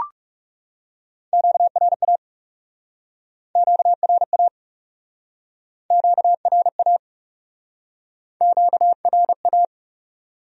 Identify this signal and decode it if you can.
Morse code, heard as a keyed tone, about 40 wpm, 700 Hz. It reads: QRA QRA QRA QRA